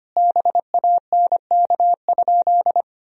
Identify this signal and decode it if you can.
Morse code, heard as a keyed tone, about 25 wpm, 700 Hz. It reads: BANK?
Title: Sentences from Top 1000 Words - Review Part 4 - 25wpm